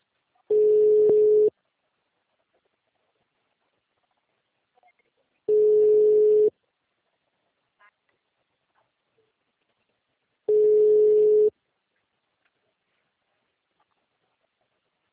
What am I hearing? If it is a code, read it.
Ringback tone (Europe, 425 Hz).